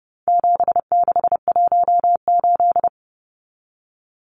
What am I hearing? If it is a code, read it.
Morse code, heard as a keyed tone, about 30 wpm, 700 Hz. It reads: 7618
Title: Numbers - 4-Digit 30wpm